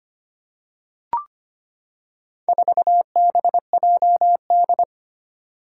Morse code, heard as a keyed tone, about 25 wpm, 700 Hz. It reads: 4BJD
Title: Four Challenging Letter-Number Combinations 25wpm